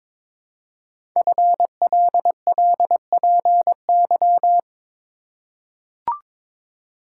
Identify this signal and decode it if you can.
Morse code, heard as a keyed tone, about 22 wpm, 700 Hz. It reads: FLLPY
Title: Code Groups - 5 Letters - 22wpm